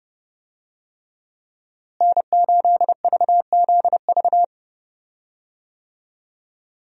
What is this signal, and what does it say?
Morse code, heard as a keyed tone, about 31 wpm, 700 Hz. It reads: N8VZV
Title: USA Call Signs - 5 Characters - Repeated 3x - 30wpm